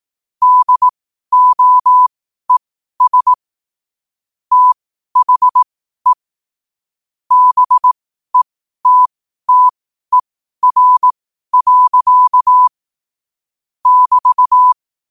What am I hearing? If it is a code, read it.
Morse code, heard as a keyed tone, about 18 wpm, spaced as for 12 wpm, 1000 Hz. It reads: DOES THE BETTER. <BT>
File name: qotdfiles / 12wpm